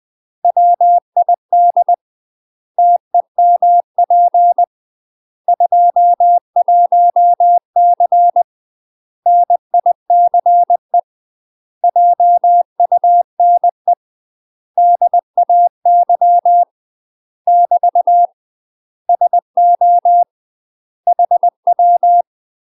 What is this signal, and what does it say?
Morse code, heard as a keyed tone, about 20 wpm, 700 Hz. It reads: WID TEMP 21C NICE JUNE DAY <BT> SO HW
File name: UK MASTER QSOs TRAINING/20WPM